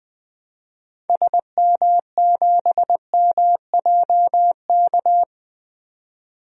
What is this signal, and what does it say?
Morse code, heard as a keyed tone, about 21 wpm, 700 Hz. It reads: SM7MJK
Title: DXCC-100 Call Signs - 6 Characters - 20wpm